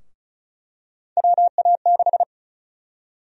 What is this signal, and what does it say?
Morse code, heard as a keyed tone, about 34 wpm, 700 Hz. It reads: WA6
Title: USA Call Sign Prefixes - 35wpm